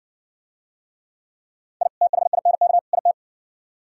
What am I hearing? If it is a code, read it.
Morse code, heard as a keyed tone, about 69 wpm, 700 Hz. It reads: I THINK IT